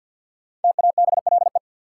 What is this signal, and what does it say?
Morse code, heard as a keyed tone, about 51 wpm, 700 Hz. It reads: TABLE
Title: Sentences from Top 400 Words - New Word Drill - 50wpm